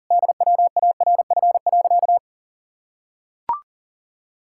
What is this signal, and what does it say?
Morse code, heard as a keyed tone, about 40 wpm, 700 Hz. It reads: DWARF.
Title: Pangram Challenge v2 - 40wpm